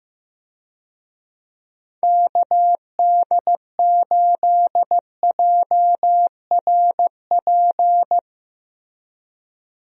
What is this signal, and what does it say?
Morse code, heard as a keyed tone, about 15 wpm, 700 Hz. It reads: KD8JRP
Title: USA Call Signs - 6 Characters - Repeated 3x - 15wpm